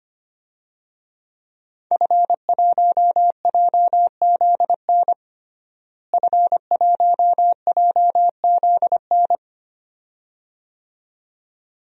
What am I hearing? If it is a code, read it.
Morse code, heard as a keyed tone, about 25 wpm, 700 Hz. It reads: F1JZN F1JZN
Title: DXCC-100 Call Signs - 5 Characters - Repeated 2x - 25wpm